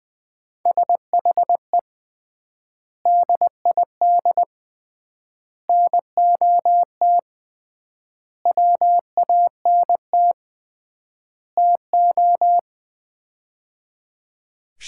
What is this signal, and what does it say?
Morse code, heard as a keyed tone, about 20 wpm, 700 Hz. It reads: SHE DID NOT WANT TO
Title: 3x Word-Spacing - Sets of 5 Words - 20wpm